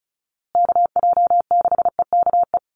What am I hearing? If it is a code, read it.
Morse code, heard as a keyed tone, about 34 wpm, 700 Hz. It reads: KJ6EKE
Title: USA Call Signs Repeated 2x 35wpm